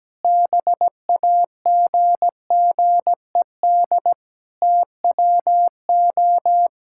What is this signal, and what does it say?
Morse code, heard as a keyed tone, about 17 wpm, 700 Hz. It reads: BAGGED TWO
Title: Pangram Challenge v2 - 17wpm